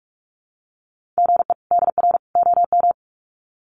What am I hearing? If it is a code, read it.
Morse code, heard as a keyed tone, about 44 wpm, 700 Hz. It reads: GE DR OM
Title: Common QSO Phrases 45wpm